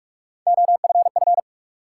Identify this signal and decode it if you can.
Morse code, heard as a keyed tone, about 46 wpm, 700 Hz. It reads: OFF
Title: Sentences from Top 200 Words - New Word Drill - 45wpm